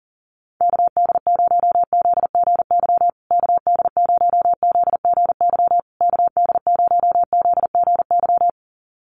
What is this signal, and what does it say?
Morse code, heard as a keyed tone, about 40 wpm, 700 Hz. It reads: KD0ZGY KD0ZGY KD0ZGY
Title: USA Call Signs Repeated 3x 40wpm